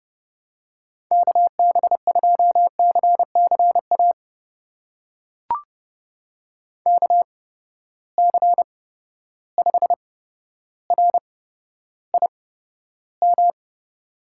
Morse code, heard as a keyed tone, about 30 wpm, spaced as for 13 wpm, 700 Hz. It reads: KB2CCA KC5RIM
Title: USA Call Signs - 6 Characters - 8x Character Spacing - 30wpm